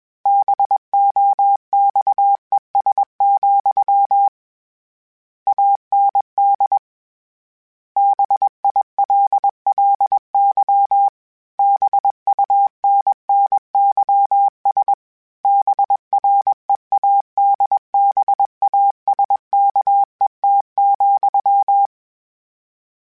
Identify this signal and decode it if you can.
Morse code, heard as a keyed tone, about 21 wpm, 800 Hz. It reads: BOXES, AND BILLY BUNNYS BREADBASKET,